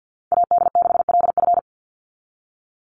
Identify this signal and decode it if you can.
Morse code, heard as a keyed tone, about 50 wpm, 700 Hz. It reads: AD6LF
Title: USA Call Signs 50wpm